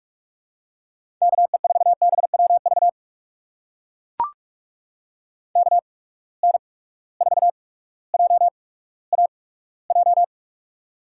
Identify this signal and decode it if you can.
Morse code, heard as a keyed tone, about 42 wpm, spaced as for 19 wpm, 700 Hz. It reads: KE4BWV KN4JAJ
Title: USA Call Signs - 6 Characters - 8x Character Spacing - 45wpm